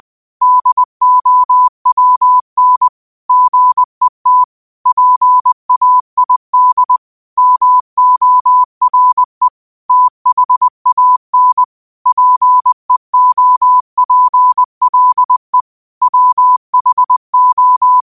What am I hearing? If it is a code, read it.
Morse code, heard as a keyed tone, about 20 wpm, 1000 Hz. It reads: DOWN GET PAID MORE THAN PEOPLE WHO